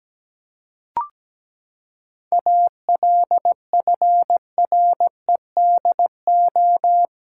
Morse code, heard as a keyed tone, about 17 wpm, 700 Hz. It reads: ALFREDO
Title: Pangram Challenge v2 - 17wpm